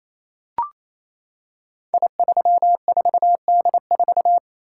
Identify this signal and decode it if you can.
Morse code, heard as a keyed tone, about 27 wpm, 700 Hz. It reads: I34D4
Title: Code Groups - 5 Letter:Numbers - 28wpm